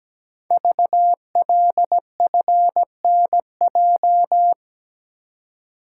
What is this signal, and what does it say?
Morse code, heard as a keyed tone, about 17 wpm, 700 Hz. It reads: VLFNJ
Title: Code Groups - 5 Letters - 17wpm